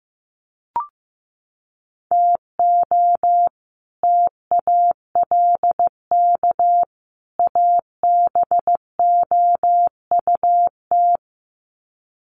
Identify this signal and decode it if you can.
Morse code, heard as a keyed tone, about 15 wpm, 700 Hz. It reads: TO TALK ABOUT
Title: Sets of 3 Words 15wpm